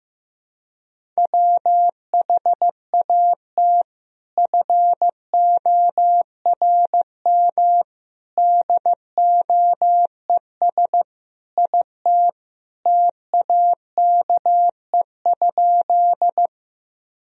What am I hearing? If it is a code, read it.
Morse code, heard as a keyed tone, about 15 wpm, 700 Hz. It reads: WHAT FORM DOES IT TAKE?
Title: Sentences from Top 1000 Words - Review Part 1 - 15wpm